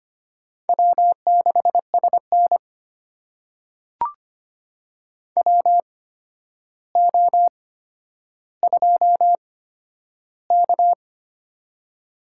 Morse code, heard as a keyed tone, about 25 wpm, spaced as for 11 wpm, 700 Hz. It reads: W6SN WO2K
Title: USA Call Signs - 4 Characters - 8x Character Spacing - 25wpm